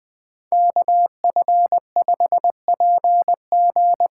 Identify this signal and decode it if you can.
Morse code, heard as a keyed tone, about 20 wpm, 700 Hz. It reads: KF5PG
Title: USA Call Signs - 6 Characters - 8x Character Spacing - 20wpm